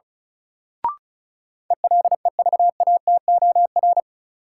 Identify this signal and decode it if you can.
Morse code, heard as a keyed tone, about 34 wpm, 700 Hz. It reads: ELEVATOR
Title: Sentences from Top 1600 Words - New Word Drill - 35wpm